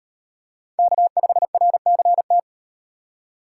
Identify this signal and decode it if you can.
Morse code, heard as a keyed tone, about 36 wpm, 700 Hz. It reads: K5RCT